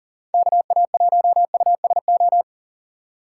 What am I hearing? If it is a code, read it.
Morse code, heard as a keyed tone, about 50 wpm, 700 Hz. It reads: KA1USO
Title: Speed Racing - USA Call Signs 40wpm